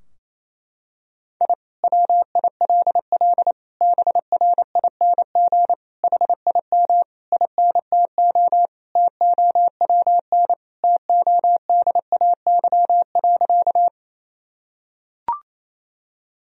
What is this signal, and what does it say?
Morse code, heard as a keyed tone, about 28 wpm, 700 Hz. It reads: I WILL BRING HIM INTO TOWN TODAY.
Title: Sentences from Top 1000 Words - Review Part 3 - 28wpm